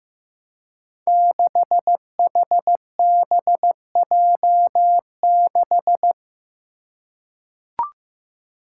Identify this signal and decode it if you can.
Morse code, heard as a keyed tone, about 15 wpm, 700 Hz. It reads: 6HBJ6
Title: Five Challenging Letter-Number Combinations 15wpm